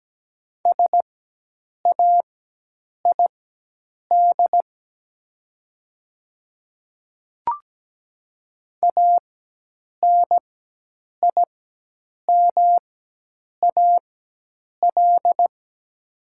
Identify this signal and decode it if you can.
Morse code, heard as a keyed tone, about 17 wpm, spaced as for 8 wpm, 700 Hz. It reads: SAID ANIMAL
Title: 4x Character Spacing - Top 200 Words - 17wpm